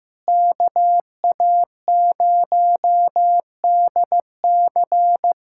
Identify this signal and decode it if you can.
Morse code, heard as a keyed tone, about 15 wpm, 700 Hz. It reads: KA0DC
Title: Practice QSO - Exchange 4 - Ending - 15wpm